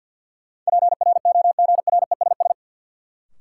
Morse code, heard as a keyed tone, about 53 wpm, 700 Hz. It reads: PROGRESS
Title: Sentences from Top 1400 Words - New Word Drill Alt - 50wpm